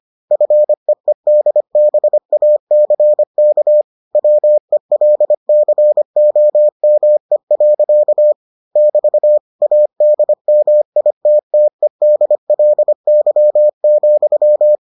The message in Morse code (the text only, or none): FEEDBACK WELCOME. <BT> ADMITTEDLY,